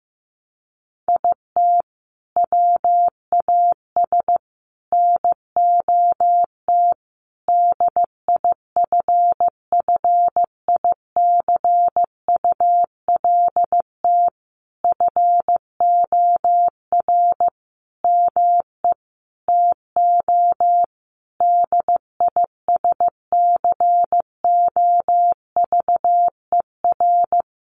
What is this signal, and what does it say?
Morse code, heard as a keyed tone, about 15 wpm, 700 Hz. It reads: IT WAS NOT DIFFICULT FOR ME TO DISCOVER
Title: Sentences from Top 1200 Words 15wpm